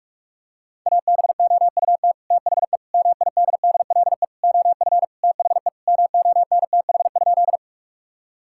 Morse code, heard as a keyed tone, about 46 wpm, 700 Hz. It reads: ABOUT THE MIDDLE OF THE MONTH?
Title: Sentences from Top 1000 Words - Review Part 6 - 45wpm